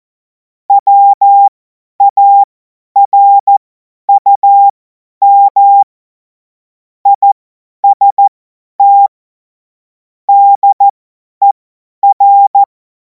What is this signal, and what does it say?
Morse code, heard as a keyed tone, about 14 wpm, spaced as for 10 wpm, 800 Hz. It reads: WARUM IST DER
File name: Morsekurs / Witze / Witz 02. Juni 2026 50 F70